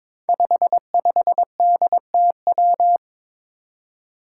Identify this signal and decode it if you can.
Morse code, heard as a keyed tone, about 22 wpm, 700 Hz. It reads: 55DTW
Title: Five Challenging Letter-Number Combinations 22wpm